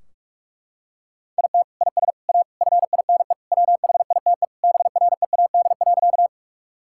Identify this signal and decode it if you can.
Morse code, heard as a keyed tone, about 41 wpm, 700 Hz. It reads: IT IS A FINE WHITE BREAD.
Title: Sentences from Top 1600 Words 45wpm